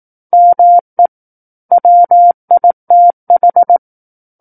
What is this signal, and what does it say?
Morse code, heard as a keyed tone, about 18 wpm, 700 Hz. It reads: ME WITH
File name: Book One / 09-Chapter 9 - 18 WPM